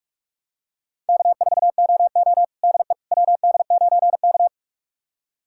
Morse code, heard as a keyed tone, about 46 wpm, 700 Hz. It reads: K4OO DE WD9K
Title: USA Callsign DE Callsign - 4 Characters - 45wpm